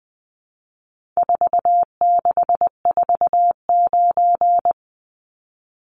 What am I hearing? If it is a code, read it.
Morse code, heard as a keyed tone, about 20 wpm, 700 Hz. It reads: 4649
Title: Numbers - 4-Digit 20wpm